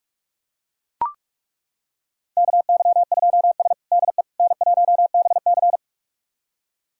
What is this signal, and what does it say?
Morse code, heard as a keyed tone, about 42 wpm, 700 Hz. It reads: KY2S DE N1BC
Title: USA Callsign DE Callsign - 4 Characters - 45wpm